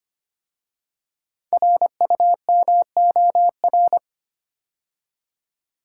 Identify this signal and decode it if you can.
Morse code, heard as a keyed tone, about 25 wpm, 700 Hz. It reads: RUMOR
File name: 5 Letter Words - First 3 Letters Given - 25wpm